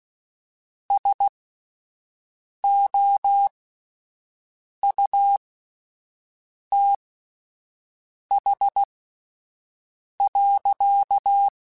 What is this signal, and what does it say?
Morse code, heard as a keyed tone, about 16 wpm, spaced as for 5 wpm, 800 Hz. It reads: SOUTH.
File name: Spaced Characters / Spaced Characters-mp3s / Weather4 21 sp @10